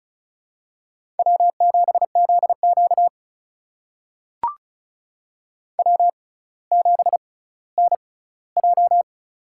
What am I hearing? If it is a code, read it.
Morse code, heard as a keyed tone, about 35 wpm, spaced as for 18 wpm, 700 Hz. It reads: W7ZQ W7NJ
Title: USA Call Signs - 6x Character Spacing - 35wpm